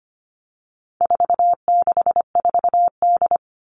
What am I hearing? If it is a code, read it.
Morse code, heard as a keyed tone, about 25 wpm, 700 Hz. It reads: 464D
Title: All Cut Numbers - 4-Digit 25wpm